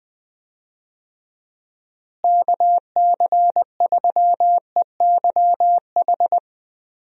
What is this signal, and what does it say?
Morse code, heard as a keyed tone, about 20 wpm, 700 Hz. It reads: KC3EYH
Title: USA Call Signs - 6 Characters - Repeated 2x - 20wpm